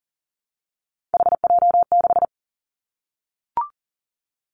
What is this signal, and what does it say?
Morse code, heard as a keyed tone, about 41 wpm, 700 Hz. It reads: HJ6
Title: Callsign Prefixes for Top 100 DXCC - 40wpm